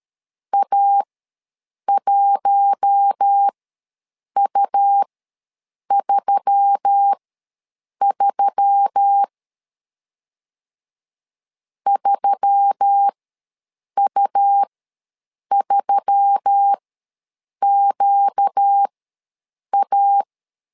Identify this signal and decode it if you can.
Morse code, heard as a keyed tone, about 13 wpm, spaced as for 7 wpm, 800 Hz. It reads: A1U33 3U3QA
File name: LEZIONI / 4 Q1S3 ws 12wpm 2038